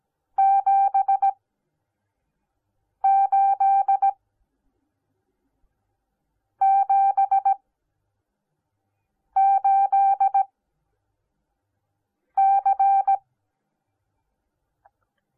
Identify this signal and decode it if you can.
Morse code, heard as a keyed tone, about 17 wpm, spaced as for 4 wpm, 780 Hz. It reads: 7878C